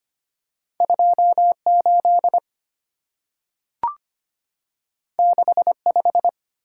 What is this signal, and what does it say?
Morse code, heard as a keyed tone, about 25 wpm, 690 Hz. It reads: 28 65
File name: Two Letter-Number Combinations - 25wpm